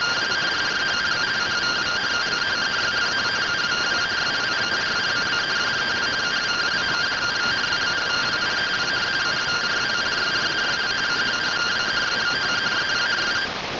Two-tone teleprinter signal, about 170 Hz shift.
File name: findings / CT SAGITAL 27.gen-16 rtty rx